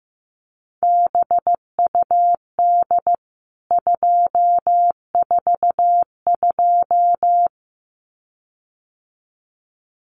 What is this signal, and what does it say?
Morse code, heard as a keyed tone, about 15 wpm, 700 Hz. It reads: BUD 242
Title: CWT Exchange 15wpm